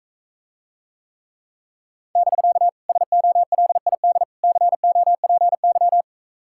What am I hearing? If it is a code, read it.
Morse code, heard as a keyed tone, about 44 wpm, 700 Hz. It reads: <BK> SOLID COPY